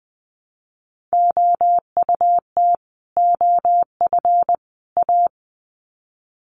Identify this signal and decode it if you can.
Morse code, heard as a keyed tone, about 20 wpm, 700 Hz. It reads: OUT OF A
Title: Sets of 3 Words 20wpm